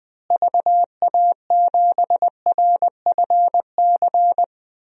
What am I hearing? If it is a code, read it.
Morse code, heard as a keyed tone, about 20 wpm, 700 Hz. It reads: VA7RFC
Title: DXCC-100 Call Signs - 6 Characters - Repeated 3x - 20wpm